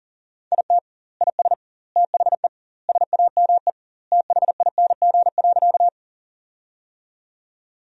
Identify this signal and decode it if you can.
Morse code, heard as a keyed tone, about 40 wpm, 700 Hz. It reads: IT IS THE SAME THING.
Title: 2x Word-Spacing - Sentences from Top 200 Words - 40wpm